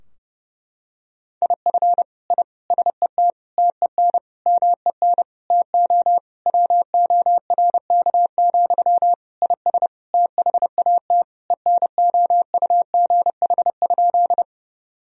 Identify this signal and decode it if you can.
Morse code, heard as a keyed tone, about 32 wpm, 700 Hz. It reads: IF I SET TEN MEN TO WORK, IS THAT ENOUGH?
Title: Sentences from Top 1000 Words - Review Part 2 - 30wpm